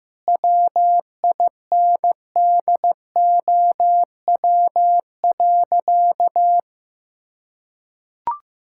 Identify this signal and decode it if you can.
Morse code, heard as a keyed tone, about 15 wpm, 700 Hz. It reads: WINDOW.
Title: Sentences from Top 1600 Words 15wpm